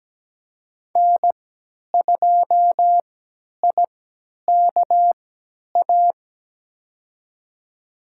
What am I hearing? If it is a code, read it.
Morse code, heard as a keyed tone, about 17 wpm, spaced as for 10 wpm, 700 Hz. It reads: N2IKA
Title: USA Call Signs - 3x Character Spacing - 17wpm